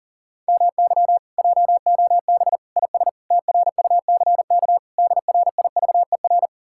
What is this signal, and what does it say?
Morse code, heard as a keyed tone, about 49 wpm, 700 Hz. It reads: MY JOB IS TRUCK DRIVER
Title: Speed-Racing Common QSO Phrases Encore - 40wpm